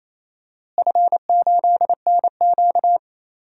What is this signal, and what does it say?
Morse code, heard as a keyed tone, about 28 wpm, 700 Hz. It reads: F8NQ
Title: DXCC-100 Call Signs - 4 Characters - Repeated 3x - 28wpm